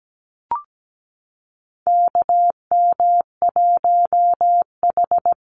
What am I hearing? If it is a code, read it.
Morse code, heard as a keyed tone, about 17 wpm, 700 Hz. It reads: KM1H
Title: USA Call Signs Repeated 3x 17wpm